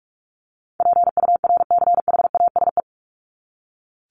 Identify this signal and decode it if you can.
Morse code, heard as a keyed tone, about 44 wpm, 700 Hz. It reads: PURCHASE
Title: Top 501-750 Words 45wpm